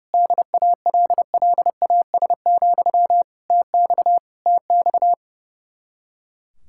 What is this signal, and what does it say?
Morse code, heard as a keyed tone, about 36 wpm, 700 Hz. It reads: DALLAS, TX TX
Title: Speed-Racing Common QSO Phrases - 30wpm